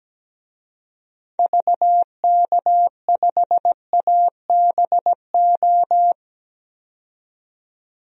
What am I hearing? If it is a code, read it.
Morse code, heard as a keyed tone, about 17 wpm, 700 Hz. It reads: VK5ABO
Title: DXCC-100 Call Signs - 6 Characters - 17wpm